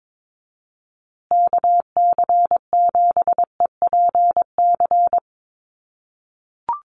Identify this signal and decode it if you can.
Morse code, heard as a keyed tone, about 22 wpm, 700 Hz. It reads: KC7EPC
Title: USA Call Signs Repeated 3x 22wpm